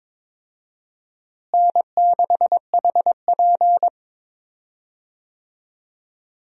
Morse code, heard as a keyed tone, about 21 wpm, 700 Hz. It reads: N6HP
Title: USA Call Signs - 4 Characters - Repeated 2x - 22wpm